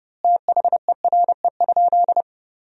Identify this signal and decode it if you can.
Morse code, heard as a keyed tone, about 30 wpm, 700 Hz. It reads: THERE?
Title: 2x Word-Spacing - Sentences from Top 200 Words - 30wpm